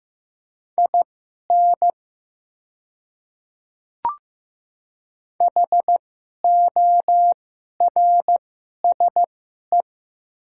Morse code, heard as a keyed tone, about 15 wpm, spaced as for 11 wpm, 700 Hz. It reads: IN HORSE
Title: 2x Character Spacing - Top 300 Words - 15wpm